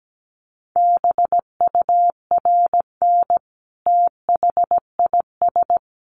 Morse code, heard as a keyed tone, about 17 wpm, 700 Hz. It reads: BURN THIS
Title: Sentences from Top 1300 Words 17wpm